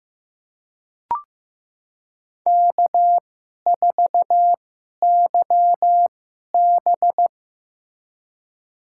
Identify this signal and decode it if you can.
Morse code, heard as a keyed tone, about 15 wpm, spaced as for 11 wpm, 700 Hz. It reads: K4YB
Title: USA Call Signs - 4 Characters - 2x Character Spacing - 15wpm